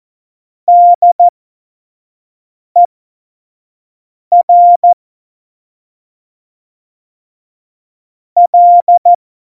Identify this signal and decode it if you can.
Morse code, heard as a keyed tone, about 14 wpm, spaced as for 5 wpm, 700 Hz. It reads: DER L
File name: Morsekurs / Witze / Witz 05. Januar 2021, 25 F70